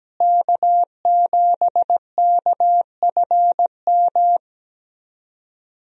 Morse code, heard as a keyed tone, about 17 wpm, 700 Hz. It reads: K7KFM
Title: USA Call Signs - 5 Characters - Repeated 2x - 17wpm